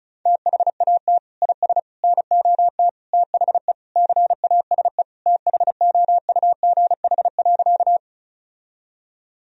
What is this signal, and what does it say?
Morse code, heard as a keyed tone, about 36 wpm, 700 Hz. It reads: THAT IS NOT THE CASE THOUGH.
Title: Sentences from Top 1000 Words - Review Part 2 - 35wpm